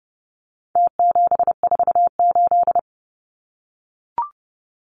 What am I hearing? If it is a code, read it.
Morse code, heard as a keyed tone, about 29 wpm, 700 Hz. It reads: T748
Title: Numbers - Contest Style - 4-Digit 30wpm